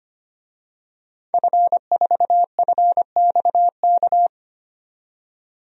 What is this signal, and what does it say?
Morse code, heard as a keyed tone, about 25 wpm, 700 Hz. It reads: F4FXK
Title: DXCC-100 Call Signs - 5 Characters - 25wpm